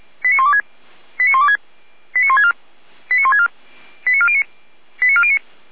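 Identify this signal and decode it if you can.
Five-tone selective call (ZVEI), heard as quick runs of five tones, tones of about 70 ms. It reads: 89126 89126 89164 89164 89409 89409
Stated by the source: ZVEI five-tone selective call received by SDR.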